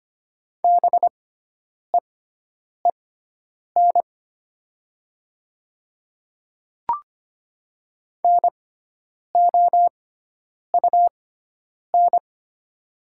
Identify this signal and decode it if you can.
Morse code, heard as a keyed tone, about 25 wpm, spaced as for 9 wpm, 700 Hz. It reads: BEEN NOUN
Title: 6x Character Spacing - Top 500 Words - 25wpm